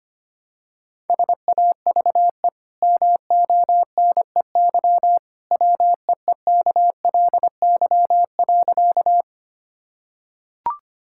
Morse code, heard as a keyed tone, about 25 wpm, 700 Hz. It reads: SAVE MONEY WEEKLY.
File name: Easy Sentences with Hints - 25wpm